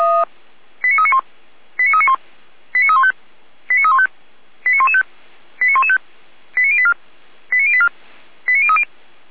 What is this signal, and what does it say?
Five-tone selective call (ZVEI), heard as quick runs of five tones, tones of about 70 ms. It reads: 89391 89391 89315 89315 89115 89115 89084 89084 89030
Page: SDR/ZVEI/89092